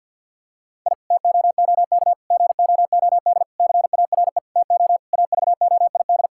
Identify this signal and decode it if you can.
Morse code, heard as a keyed tone, about 52 wpm, 700 Hz. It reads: I TOOK GOOD CARE TO AVOID
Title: Sentences from Top 1000 Words - Review Part 6 - 50wpm